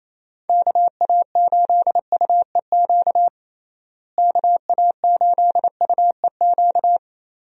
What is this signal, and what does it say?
Morse code, heard as a keyed tone, about 28 wpm, 700 Hz. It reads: KA8UEQ KA8UEQ
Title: USA Call Signs - 6 Characters - Repeated 3x - 28wpm